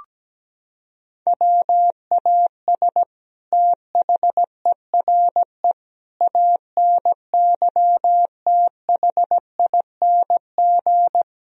Morse code, heard as a keyed tone, about 17 wpm, 700 Hz. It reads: WAS THERE ANYTHING
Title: Sentences from Top 1000 Words - Review Part 3 - 17wpm